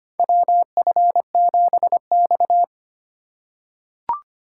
Morse code, heard as a keyed tone, about 25 wpm, 700 Hz. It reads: WF7X
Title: USA Call Signs - 4 Characters - 2x Character Spacing - 25wpm